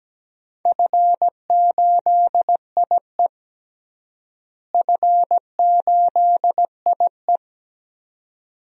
Morse code, heard as a keyed tone, about 17 wpm, 700 Hz. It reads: F8IE F8IE
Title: DXCC-100 Call Signs - 4 Characters - Repeated 2x - 17wpm